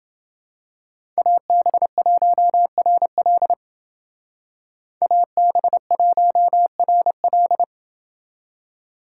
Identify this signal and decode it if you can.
Morse code, heard as a keyed tone, about 29 wpm, 700 Hz. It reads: AB1RL AB1RL